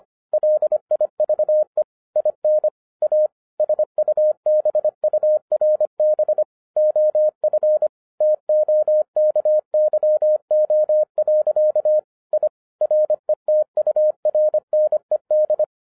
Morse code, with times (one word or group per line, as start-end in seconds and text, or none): 0.3-1.8 LIVE
2.2-2.7 IN
3.0-3.3 A
3.6-6.4 SUBURB
6.8-7.9 OF
8.2-12.0 TOKYO.
12.3-12.5 I
12.8-15.6 RETURNED